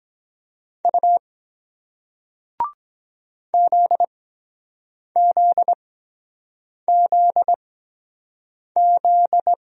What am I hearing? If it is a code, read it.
Morse code, heard as a keyed tone, about 23 wpm, spaced as for 7 wpm, 700 Hz. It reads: U ZZZZ